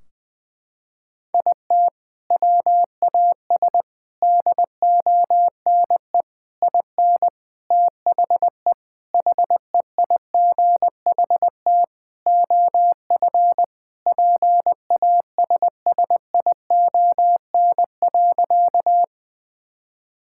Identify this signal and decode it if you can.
Morse code, heard as a keyed tone, about 20 wpm, 700 Hz. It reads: IT WAS DONE IN THE HEIGHT OF PASSION.
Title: Sentences from Top 1600 Words 20wpm